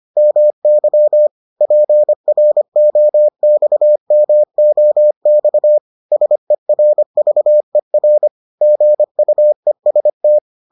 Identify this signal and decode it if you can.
Morse code, heard as a keyed tone, about 25 wpm, 600 Hz. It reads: MY PROXMOX SERVER GUEST